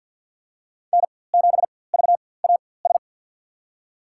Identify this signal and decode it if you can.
Morse code, heard as a keyed tone, about 53 wpm, spaced as for 24 wpm, 700 Hz. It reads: N7VAS
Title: USA Call Signs - 5 Characters - 4x Character Spacing - 50wpm